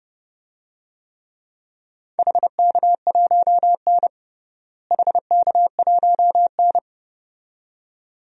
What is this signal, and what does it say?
Morse code, heard as a keyed tone, about 30 wpm, 700 Hz. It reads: HK1N HK1N
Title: DXCC-100 Call Signs - 4 Characters - Repeated 2x - 30wpm